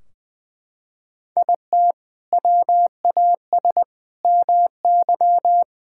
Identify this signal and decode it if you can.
Morse code, heard as a keyed tone, about 19 wpm, 700 Hz. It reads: IT WAS MY
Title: Sentences from Top 1600 Words 20wpm